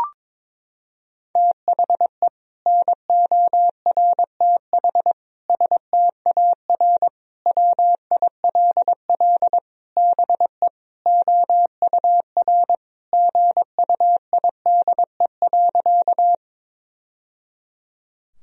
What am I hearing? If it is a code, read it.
Morse code, heard as a keyed tone, about 22 wpm, 700 Hz. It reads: THE NORTH STAR WILL BE OUR GUIDE.
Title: Sentences from Top 1000 Words - Review Part 3 - 22wpm